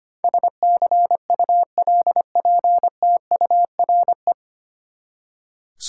Morse code, heard as a keyed tone, about 25 wpm, 700 Hz. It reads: SCULPTURE
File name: Nine Letter Words - 25wpm